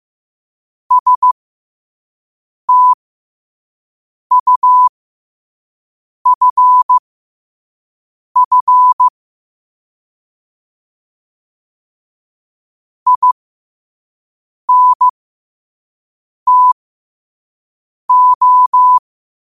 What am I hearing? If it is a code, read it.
Morse code, heard as a keyed tone, about 15 wpm, spaced as for 5 wpm, 1000 Hz. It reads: STUFF INTO
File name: qotdfiles / 5wpm